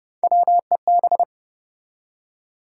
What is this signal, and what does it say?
Morse code, heard as a keyed tone, about 28 wpm, 700 Hz. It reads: WEB
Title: Three Letter Words - 30wpm